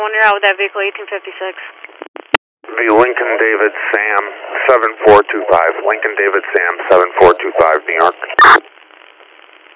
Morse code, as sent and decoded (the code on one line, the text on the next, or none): none